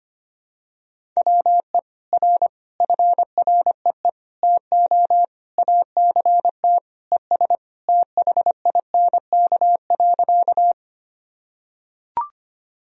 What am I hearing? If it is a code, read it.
Morse code, heard as a keyed tone, about 25 wpm, 700 Hz. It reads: WE R FREE TO ACT ES THINK.
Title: Sentences from Top 500 Words and Abbreviations 25wpm